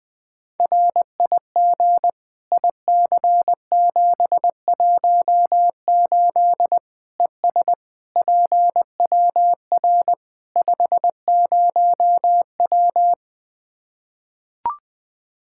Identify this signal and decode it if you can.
Morse code, heard as a keyed tone, about 20 wpm, 700 Hz. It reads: RIG IC718 ES PWR 50W
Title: Practice QSO - Exchange 2 - Info - 20wpm